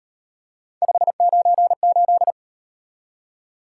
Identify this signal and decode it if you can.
Morse code, heard as a keyed tone, about 37 wpm, 700 Hz. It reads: 598